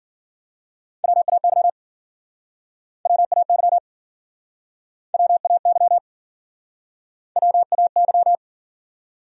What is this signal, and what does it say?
Morse code, heard as a keyed tone, about 53 wpm, 700 Hz. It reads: WAY WAY WAY WAY